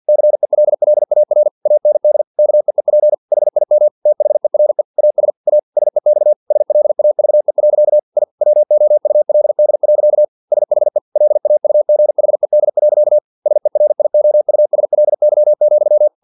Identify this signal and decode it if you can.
Morse code, heard as a keyed tone, about 50 wpm, 600 Hz. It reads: CELLAR AND KEEP HIM THERE AS A SEX SLAVE. I WOULD. SHE LAUGHED. SERIOUSLY,
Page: Never 49wpm